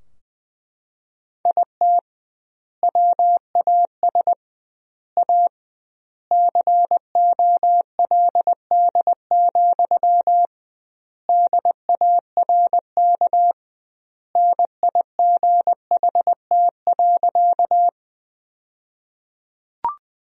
Morse code, heard as a keyed tone, about 20 wpm, 700 Hz. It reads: IT WAS A COLD, DARK NIGHT.
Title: 2x Word-Spacing - Sentences from Top 500 Words - 20wpm